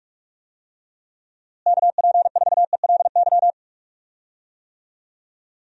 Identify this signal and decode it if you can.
Morse code, heard as a keyed tone, about 46 wpm, 700 Hz. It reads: KP4ELY
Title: USA Call Signs - 6 Characters - Repeated 3x - 45wpm